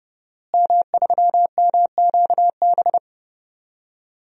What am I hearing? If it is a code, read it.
Morse code, heard as a keyed tone, about 30 wpm, 700 Hz. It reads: M3MQB
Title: DXCC-100 Call Signs - 5 Characters - Repeated 3x - 30wpm